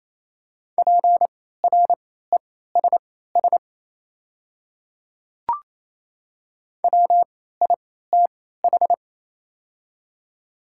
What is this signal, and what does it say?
Morse code, heard as a keyed tone, about 27 wpm, spaced as for 16 wpm, 690 Hz. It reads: PRESS WITH
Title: 3x Character Spacing - Top 500 Words - 28wpm